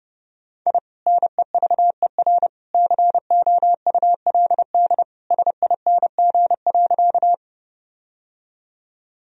Morse code, heard as a keyed tone, about 32 wpm, 700 Hz. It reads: I NEVER COULD SING.
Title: Sentences from Top 1000 Words - Review Part 2 - 30wpm